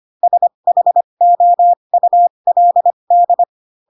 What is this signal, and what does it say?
Morse code, heard as a keyed tone, about 25 wpm, spaced as for 22 wpm, 700 Hz. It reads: SHOULD
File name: Pinocchio / Pinocchio ch7 18 25 ik1hge